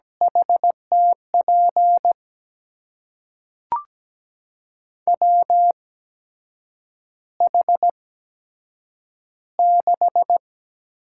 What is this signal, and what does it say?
Morse code, heard as a keyed tone, about 17 wpm, spaced as for 7 wpm, 700 Hz. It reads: HTP WH6
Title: USA Call Signs - 6 Characters - 8x Character Spacing - 17wpm